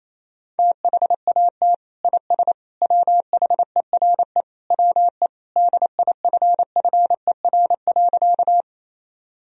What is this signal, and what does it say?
Morse code, heard as a keyed tone, about 28 wpm, 700 Hz. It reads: THAT IS WHERE WE DIFFER.
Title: Sentences from Top 1000 Words - Review Part 1 - 28wpm